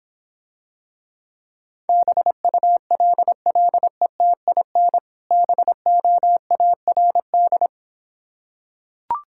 Morse code, heard as a keyed tone, about 26 wpm, 700 Hz. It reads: BULLETIN BOARD